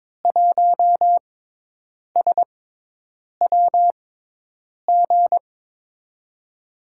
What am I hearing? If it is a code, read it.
Morse code, heard as a keyed tone, about 23 wpm, spaced as for 8 wpm, 700 Hz. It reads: 1SWG
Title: USA Call Signs - 5 Characters - 6x Character Spacing - 22wpm